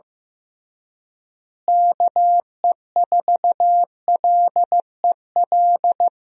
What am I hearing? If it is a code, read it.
Morse code, heard as a keyed tone, about 16 wpm, 700 Hz. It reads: KE4LEL
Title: USA Call Signs - 6 Characters - Repeated 3x - 15wpm